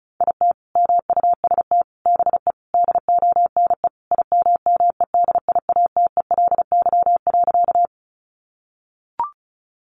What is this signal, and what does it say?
Morse code, heard as a keyed tone, about 34 wpm, 700 Hz. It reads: IT MUST BE DONE IMMEDIATELY.
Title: Sentences from Top 1100 Words 35wpm